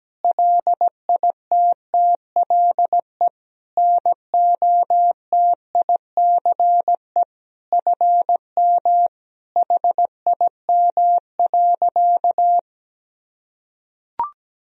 Morse code, heard as a keyed tone, about 17 wpm, 700 Hz. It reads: LITTLE NOTICE FM HIM.
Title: Sentences from Top 400 Words and Abbreviations 17wpm